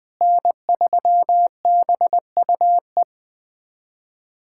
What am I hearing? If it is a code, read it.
Morse code, heard as a keyed tone, about 20 wpm, 700 Hz. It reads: N3BUE
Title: USA Callsign DE Callsign - 5 Characters - 20wpm